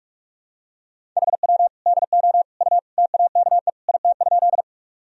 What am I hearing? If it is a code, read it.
Morse code, heard as a keyed tone, about 46 wpm, 700 Hz. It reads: HW DO U TAKE IT?
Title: Sentences from Top 200 Words and Abbreviations 45wpm